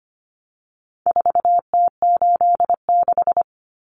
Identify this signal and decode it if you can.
Morse code, heard as a keyed tone, about 25 wpm, 700 Hz. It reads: 4T86
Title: Common Cut Numbers - 4-Digit 25wpm